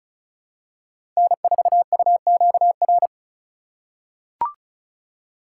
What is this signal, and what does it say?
Morse code, heard as a keyed tone, about 36 wpm, 700 Hz. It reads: N4UQR
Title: USA Call Signs - 5 Characters - 8x Character Spacing - 35wpm